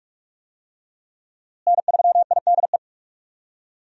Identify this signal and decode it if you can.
Morse code, heard as a keyed tone, about 46 wpm, 700 Hz. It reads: N3IDE
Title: USA Call Signs - 5 Characters - Repeated 2x - 45wpm